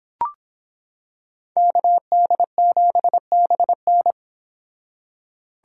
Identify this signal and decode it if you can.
Morse code, heard as a keyed tone, about 26 wpm, 700 Hz. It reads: KD7BN